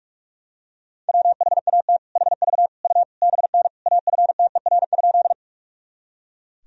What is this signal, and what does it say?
Morse code, heard as a keyed tone, about 41 wpm, 700 Hz. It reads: WHAT HV U BN AFTER?
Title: Sentences from Top 200 Words and Abbreviations 45wpm